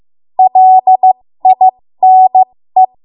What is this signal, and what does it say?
Morse code, heard as a keyed tone, about 15 wpm, spaced as for 13 wpm, 750 Hz. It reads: LINE
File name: Morse / 13WPM